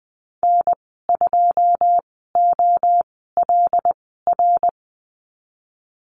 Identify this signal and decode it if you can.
Morse code, heard as a keyed tone, about 20 wpm, spaced as for 15 wpm, 700 Hz. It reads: N2OLR
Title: 179 Lesson 29g - Slash - Call Signs 20wpm